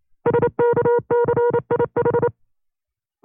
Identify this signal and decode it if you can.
Morse code, heard as a keyed tone, about 29 wpm, 480 Hz. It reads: SKCIH
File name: FreeFiles / CW-practice / 25WPM-side2